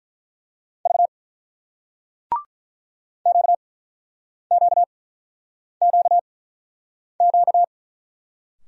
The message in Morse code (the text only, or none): V QQQQ